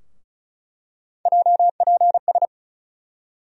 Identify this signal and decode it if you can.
Morse code, heard as a keyed tone, about 36 wpm, 700 Hz. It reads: JPS
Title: USA Call Sign Postfixes - 3 Letters 35wpm